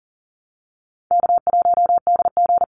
Morse code, heard as a keyed tone, about 37 wpm, 700 Hz. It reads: K1DG
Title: Call Signs Repeated 2x - 40wpm